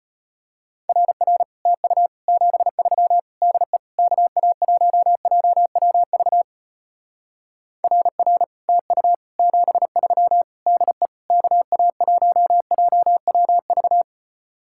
Morse code, heard as a keyed tone, about 33 wpm, 700 Hz. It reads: RR TU 73 DE KA1JWV RR TU 73 DE KA1JWV